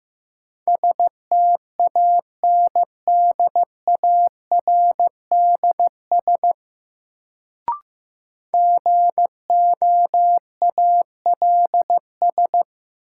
Morse code, heard as a keyed tone, about 16 wpm, 700 Hz. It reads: STANDARDS GOALS
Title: Sentences from Top 1000 Words - Review Part 6 - New Word Drill - 15wpm